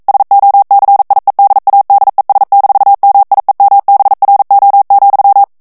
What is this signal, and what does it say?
Morse code, heard as a keyed tone, about 42 wpm, 800 Hz. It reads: SOCIEDADES-MIEMBRO,
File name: morse mp3 / ESPAGNOL TEXTE EN CLAIR 1 VITESSE 40 MOTS MINUTES